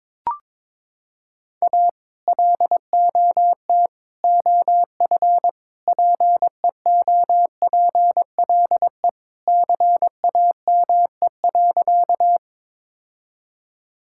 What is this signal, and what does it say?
Morse code, heard as a keyed tone, about 21 wpm, 700 Hz. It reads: A LOT OF PEOPLE CAME.
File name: practice / 22wpm / Sentences with 3-Word Phrases - 22wpm